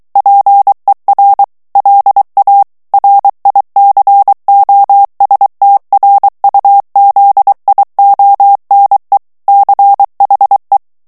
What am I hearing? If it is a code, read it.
Morse code, heard as a keyed tone, about 23 wpm, 800 Hz. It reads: PER LA RICOSTRUZIONE CHE
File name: morse mp3 / ITALIEN texte clair 1 vitesse 22 mots minute